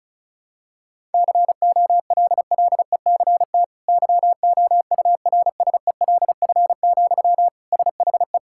Morse code, heard as a keyed tone, about 34 wpm, 700 Hz. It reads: COLLECT YOURSELF, SHE
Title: Sentences from Top 1600 Words 35wpm